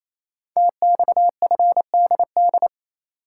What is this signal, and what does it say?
Morse code, heard as a keyed tone, about 28 wpm, 700 Hz. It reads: TXFDD
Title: Code Groups - 5 Letters - 28wpm